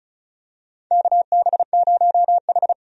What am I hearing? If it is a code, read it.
Morse code, heard as a keyed tone, about 36 wpm, 700 Hz. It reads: KB0H
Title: USA Call Signs - 4 Characters - 35wpm